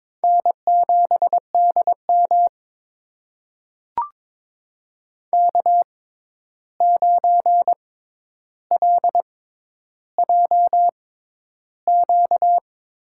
Morse code, heard as a keyed tone, about 22 wpm, spaced as for 11 wpm, 700 Hz. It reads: N7DM K9LJQ
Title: USA Call Signs - 6x Character Spacing - 22wpm